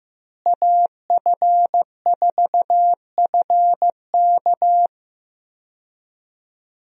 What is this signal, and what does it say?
Morse code, heard as a keyed tone, about 15 wpm, 700 Hz. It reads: AF4FK
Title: USA Call Signs - 5 Characters - Repeated 3x - 15wpm